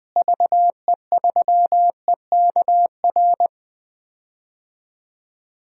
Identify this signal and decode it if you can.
Morse code, heard as a keyed tone, about 19 wpm, 700 Hz. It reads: VE3EKR
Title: DXCC-100 Call Signs - 6 Characters - Repeated 3x - 20wpm